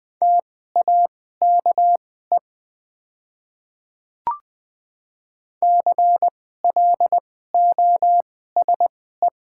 Morse code, heard as a keyed tone, about 20 wpm, spaced as for 14 wpm, 700 Hz. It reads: TAKE CLOSE
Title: 2x Character Spacing - Top 500 Words - 20wpm